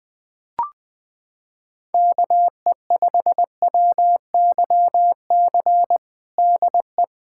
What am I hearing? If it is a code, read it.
Morse code, heard as a keyed tone, about 20 wpm, 700 Hz. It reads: KE5WYC DE
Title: USA Callsign DE Callsign - 6 Characters - 20wpm